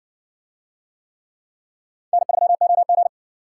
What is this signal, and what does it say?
Morse code, heard as a keyed tone, about 59 wpm, 700 Hz. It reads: N3OG